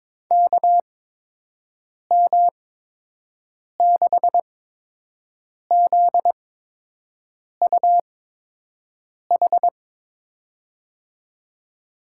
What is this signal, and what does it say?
Morse code, heard as a keyed tone, about 22 wpm, spaced as for 6 wpm, 700 Hz. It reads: KM6ZUH
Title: USA Call Signs - 6 Characters - 8x Character Spacing - 22wpm